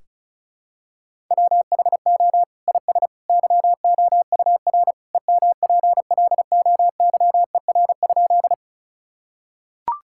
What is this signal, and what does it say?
Morse code, heard as a keyed tone, about 36 wpm, 700 Hz. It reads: WHO IS YOUR EMPLOYER?
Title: Sentences from Top 1600 Words 35wpm